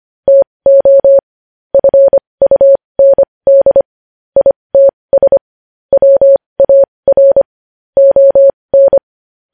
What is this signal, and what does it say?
Morse code, heard as a keyed tone, about 25 wpm, spaced as for 20 wpm, 550 Hz. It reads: TO FUND ITS WAR ON
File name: podcast / News headlines-1x-25x20